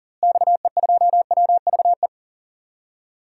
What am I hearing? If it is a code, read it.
Morse code, heard as a keyed tone, about 40 wpm, 700 Hz. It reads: XE2WVE
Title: DXCC-100 Call Signs - 6 Characters - Repeated 2x - 40wpm